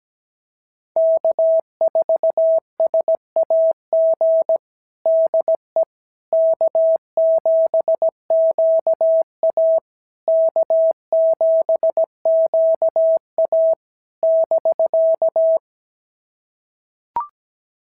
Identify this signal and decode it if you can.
Morse code, heard as a keyed tone, about 16 wpm, 650 Hz. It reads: K4SAG DE K7QA K7QA <BK>
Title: SOTA QSO Practice 17wpm